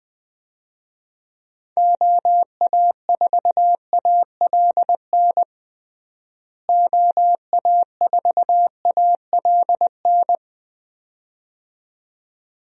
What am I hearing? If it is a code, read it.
Morse code, heard as a keyed tone, about 20 wpm, 700 Hz. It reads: OA4ALN OA4ALN
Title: USA Call Signs - 6 Characters - Repeated 2x - 20wpm